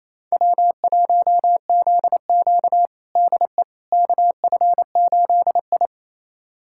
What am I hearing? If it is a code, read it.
Morse code, heard as a keyed tone, about 28 wpm, 700 Hz. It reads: W1ZQ DE KF8I
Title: USA Callsign DE Callsign - 4 Characters - 28wpm